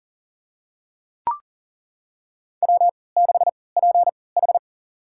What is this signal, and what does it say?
Morse code, heard as a keyed tone, about 38 wpm, spaced as for 22 wpm, 700 Hz. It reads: W6PH
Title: USA Call Signs - 4 Characters - 3x Character Spacing - 40wpm